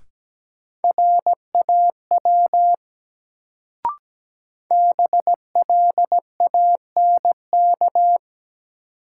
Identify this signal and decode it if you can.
Morse code, heard as a keyed tone, about 17 wpm, 700 Hz. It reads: RAW BLANK
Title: Sentences from Top 1500 Words - New Word Drill Alt - 17wpm